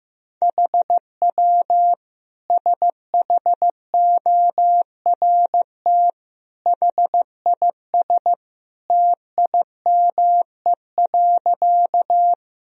Morse code, heard as a keyed tone, about 15 wpm, 700 Hz. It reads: HW SHORT HIS TIME.
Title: Sentences from Top 400 Words and Abbreviations 15wpm